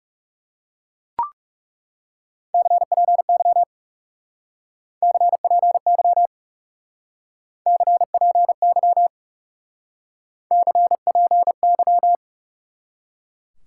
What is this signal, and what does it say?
Morse code, heard as a keyed tone, about 36 wpm, 700 Hz. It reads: CPY CPY CPY CPY
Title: Speed-Racing Top 90 QSO Elements 30wpm